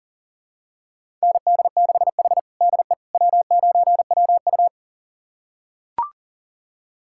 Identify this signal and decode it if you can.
Morse code, heard as a keyed tone, about 37 wpm, 700 Hz. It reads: ND6H DE W9WU
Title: USA Callsign DE Callsign - 4 Characters - 40wpm